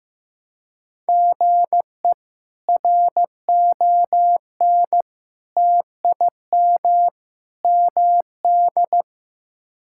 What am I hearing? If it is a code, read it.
Morse code, heard as a keyed tone, about 15 wpm, 700 Hz. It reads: GE RON TIM MD
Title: K1USN SST QSO Practice - 15wpm